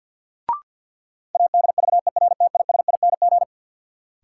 Morse code, heard as a keyed tone, about 48 wpm, 700 Hz. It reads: ADVERTISING
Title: Sentences from Top 1000 Words - Review Part 6 - New Word Drill - 50wpm